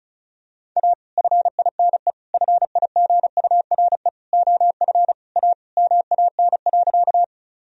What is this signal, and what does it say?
Morse code, heard as a keyed tone, about 34 wpm, 700 Hz. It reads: A FINE FIGURE OF A MAN.
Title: Sentences from Top 1000 Words - Review Part 2 - 35wpm